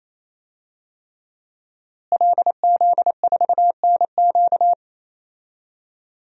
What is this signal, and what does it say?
Morse code, heard as a keyed tone, about 28 wpm, 700 Hz. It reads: LZ4NQ
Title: DXCC-100 Call Signs - 5 Characters - Repeated 3x - 28wpm